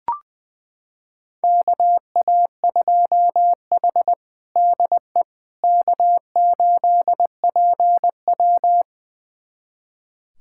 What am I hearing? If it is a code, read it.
Morse code, heard as a keyed tone, about 21 wpm, 700 Hz. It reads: KA2H DE K8PW
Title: USA Callsign DE Callsign - 4 Characters - 20wpm